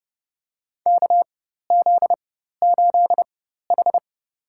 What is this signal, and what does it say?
Morse code, heard as a keyed tone, about 32 wpm, spaced as for 14 wpm, 700 Hz. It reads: KZ8H
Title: USA Call Signs - 4x Character Spacing - 30wpm